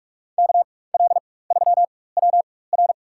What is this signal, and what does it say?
Morse code, heard as a keyed tone, about 46 wpm, spaced as for 21 wpm, 700 Hz. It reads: KL3WR
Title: USA Call Signs - 4x Character Spacing - 45wpm